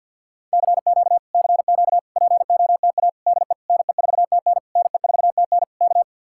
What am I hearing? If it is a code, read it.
Morse code, heard as a keyed tone, about 66 wpm, spaced as for 58 wpm, 700 Hz. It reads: CQ CQ POTA DE NE4TN NE4TN K
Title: Speed-Racing Common QSO Phrases - 50wpm